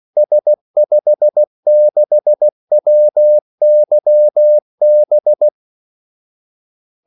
Morse code, heard as a keyed tone, about 16 wpm, 600 Hz. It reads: S56WYB
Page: Kaldesignaler 16 wpm